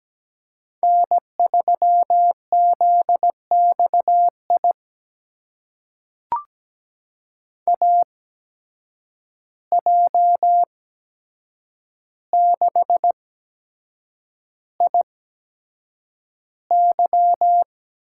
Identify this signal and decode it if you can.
Morse code, heard as a keyed tone, about 17 wpm, spaced as for 7 wpm, 700 Hz. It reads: N3ZXI AJ6IY
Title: USA Call Signs - 5 Characters - 8x Character Spacing - 17wpm